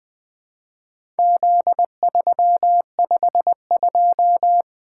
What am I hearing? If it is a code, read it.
Morse code, heard as a keyed tone, about 21 wpm, 700 Hz. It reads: Z352
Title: Four Letter-Number Combinations - 20wpm